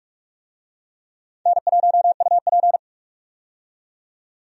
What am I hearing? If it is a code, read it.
Morse code, heard as a keyed tone, about 46 wpm, 700 Hz. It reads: N1UP
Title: USA Call Signs - 4 Characters - 45wpm